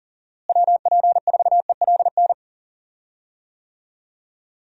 Morse code, heard as a keyed tone, about 40 wpm, 700 Hz. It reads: WP4ELN
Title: USA Call Signs - 6 Characters - Repeated 3x - 40wpm